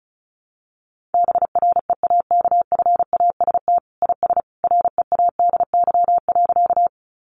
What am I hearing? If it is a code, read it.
Morse code, heard as a keyed tone, about 34 wpm, 700 Hz. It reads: BREAKFAST IS READY.
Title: Sentences from Top 1500 Words 35wpm